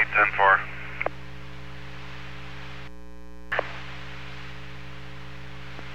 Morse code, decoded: none

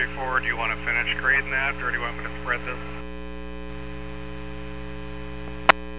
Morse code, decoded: none

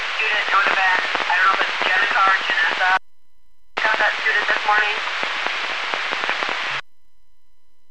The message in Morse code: none

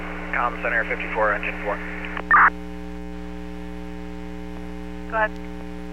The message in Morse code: none

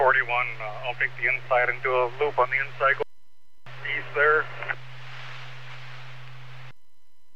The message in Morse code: none

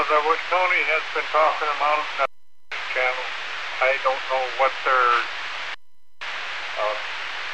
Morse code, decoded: none

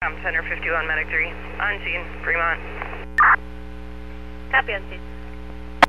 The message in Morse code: none

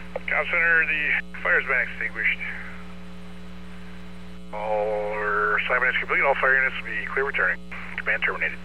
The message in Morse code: none

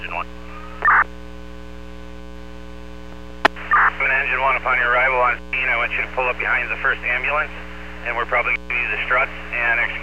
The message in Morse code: none

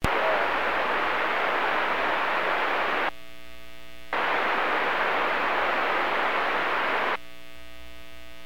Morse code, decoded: none

